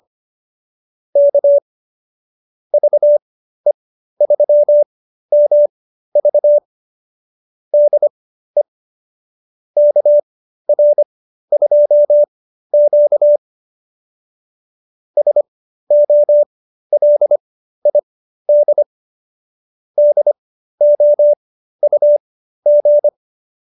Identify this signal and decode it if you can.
Morse code, heard as a keyed tone, about 25 wpm, spaced as for 13 wpm, 590 Hz. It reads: K VE3MV DE KR2Q SOLID DOUG